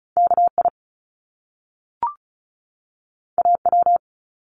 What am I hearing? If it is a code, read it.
Morse code, heard as a keyed tone, about 36 wpm, 700 Hz. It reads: KI AW
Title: Top Two Letter Combinations in Words 35wpm